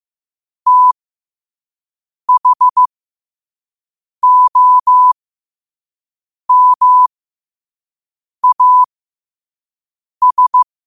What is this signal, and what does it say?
Morse code, heard as a keyed tone, about 15 wpm, spaced as for 5 wpm, 1000 Hz. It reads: THOMAS
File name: qotdfiles / 5wpm